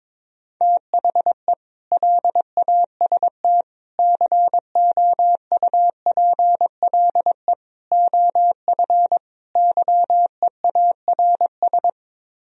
Morse code, heard as a keyed tone, about 22 wpm, 700 Hz. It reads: THE LAST COUPLE OF YEARS
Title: Speed-Racing - Sets of 5 Words 22wpm